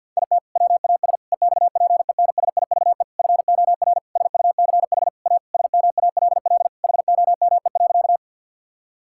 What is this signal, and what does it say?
Morse code, heard as a keyed tone, about 52 wpm, 700 Hz. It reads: IT WAS EXPENSIVE FOR SUCH A SMALL HOME.
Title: Sentences from Top 1000 Words - Review Part 6 - 50wpm